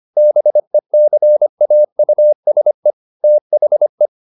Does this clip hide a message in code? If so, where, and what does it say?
Yes: Morse code from 0.2 to 4.1 s, heard as a keyed tone, about 25 wpm, 600 Hz. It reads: BECAUSE THE